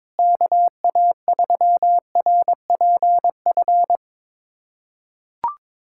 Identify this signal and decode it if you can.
Morse code, heard as a keyed tone, about 22 wpm, 700 Hz. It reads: KA3RPF
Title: USA Callsign DE Callsign 22wpm